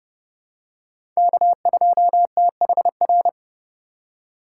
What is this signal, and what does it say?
Morse code, heard as a keyed tone, about 29 wpm, 700 Hz. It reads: K2THR
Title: USA Call Signs - 2x Character Spacing - 30wpm